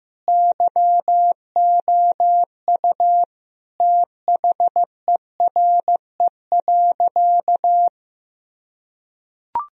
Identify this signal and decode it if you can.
Morse code, heard as a keyed tone, about 15 wpm, 700 Hz. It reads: YOU THERE.
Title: Sentences from Top 1000 Words - Review Part 3 - 15wpm